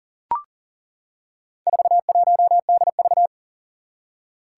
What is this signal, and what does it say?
Morse code, heard as a keyed tone, about 40 wpm, 700 Hz. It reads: 41DV
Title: Four Challenging Letter-Number Combinations 40wpm